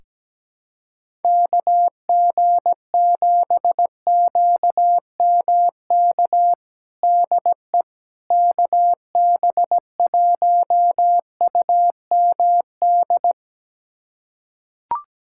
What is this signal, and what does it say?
Morse code, heard as a keyed tone, about 17 wpm, 700 Hz. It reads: KG7QMK DE KB1UMD
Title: USA Callsign DE Callsign - 6 Characters - 17wpm